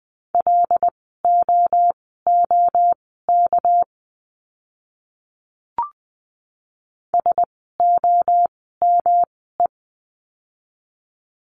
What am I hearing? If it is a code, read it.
Morse code, heard as a keyed tone, about 20 wpm, spaced as for 15 wpm, 700 Hz. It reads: LOOK SOME
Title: 230 Lesson 37f - Letter x - Words 20wpm